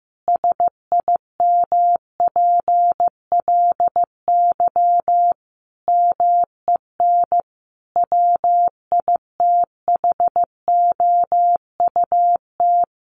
Morse code, heard as a keyed tone, about 15 wpm, 700 Hz. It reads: SIMPLY MEN WITHOUT
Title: Sentences from Top 1500 Words 15wpm